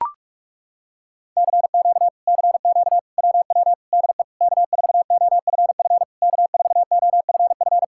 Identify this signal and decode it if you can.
Morse code, heard as a keyed tone, about 37 wpm, 700 Hz. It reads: CQ CQ WW DE K4OFF K4OFF
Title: Speed-Racing Common QSO Phrases Encore - 30wpm